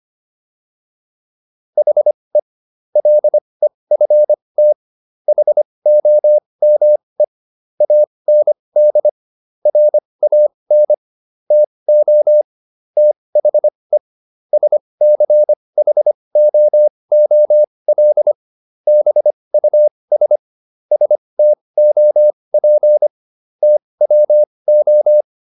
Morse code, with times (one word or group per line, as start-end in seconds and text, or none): 1.8-2.4 HE
3.0-4.7 LEFT
5.3-7.2 HOME
7.8-9.1 AND
9.7-10.9 RAN
11.5-12.4 TO
13.0-14.0 THE
14.5-18.3 SCHOOL
18.9-20.4 BUS
20.9-23.1 STOP
23.6-25.2 TWO